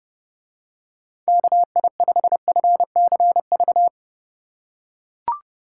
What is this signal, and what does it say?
Morse code, heard as a keyed tone, about 30 wpm, 700 Hz. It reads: KI5FCV